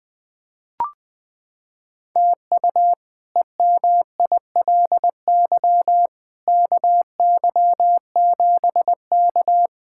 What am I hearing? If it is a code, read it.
Morse code, heard as a keyed tone, about 20 wpm, 700 Hz. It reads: TU EMILY KY7K
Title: K1USN SST QSO Practice - 20wpm